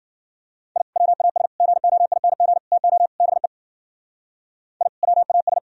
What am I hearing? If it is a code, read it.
Morse code, heard as a keyed tone, about 69 wpm, 700 Hz. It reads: I WAS GOING TO BE I WAS